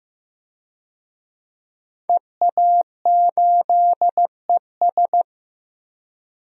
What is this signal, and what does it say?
Morse code, heard as a keyed tone, about 15 wpm, 700 Hz. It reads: EA8ES
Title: DXCC-100 Call Signs - 5 Characters - 15wpm